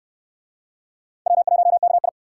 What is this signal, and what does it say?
Morse code, heard as a keyed tone, about 57 wpm, 700 Hz. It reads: W1KI